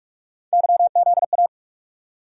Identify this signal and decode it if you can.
Morse code, heard as a keyed tone, about 47 wpm, 700 Hz. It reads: YZA
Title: USA Call Sign Postfixes - 3 Letters 45wpm